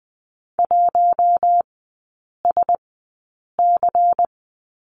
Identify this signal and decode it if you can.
Morse code, heard as a keyed tone, about 19 wpm, spaced as for 8 wpm, 700 Hz. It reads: 1SC
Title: 100 Lesson 17e - Letter u - Three Characters 20wpm